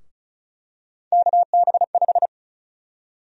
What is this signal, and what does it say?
Morse code, heard as a keyed tone, about 34 wpm, 700 Hz. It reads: KB5
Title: USA Call Sign Prefixes - Repeated 2x - 35wpm